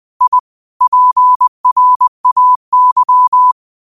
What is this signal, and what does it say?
Morse code, heard as a keyed tone, about 20 wpm, 1000 Hz. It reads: I PRAY
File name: qotdfiles / 20wpm